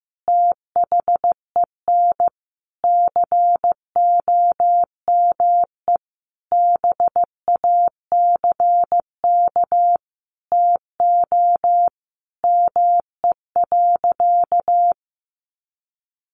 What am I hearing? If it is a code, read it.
Morse code, heard as a keyed tone, about 15 wpm, 700 Hz. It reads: THEN COME BACK TO ME.
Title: Sentences from Top 600 Words 15wpm